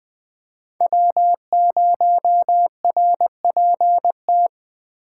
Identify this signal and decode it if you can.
Morse code, heard as a keyed tone, about 21 wpm, 700 Hz. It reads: W0RPT